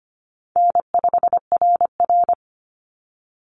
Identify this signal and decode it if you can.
Morse code, heard as a keyed tone, about 25 wpm, 700 Hz. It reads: N5RR
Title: Easy Call Signs - 25wpm